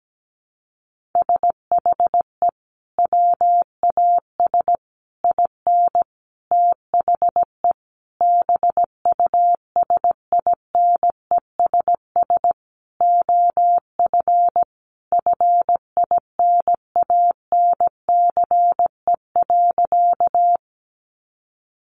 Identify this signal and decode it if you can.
Morse code, heard as a keyed tone, about 17 wpm, 700 Hz. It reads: SHE WAS IN THE BUSINESS OF FINANCE.
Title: Sentences from Top 1500 Words 17wpm